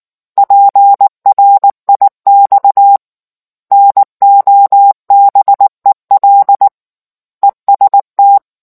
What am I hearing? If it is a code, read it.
Morse code, heard as a keyed tone, about 19 wpm, 800 Hz. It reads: PRIX NOBEL EST